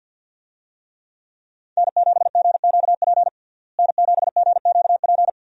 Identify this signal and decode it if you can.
Morse code, heard as a keyed tone, about 51 wpm, 700 Hz. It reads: N7GQP N7GQP
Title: USA Call Signs - 5 Characters - Repeated 3x - 50wpm